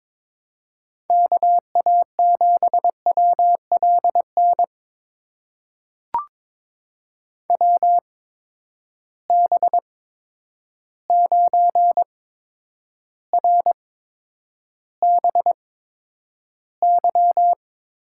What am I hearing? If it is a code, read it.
Morse code, heard as a keyed tone, about 21 wpm, spaced as for 9 wpm, 700 Hz. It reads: KA7WLN WB9RBY
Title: USA Call Signs - 6 Characters - 8x Character Spacing - 22wpm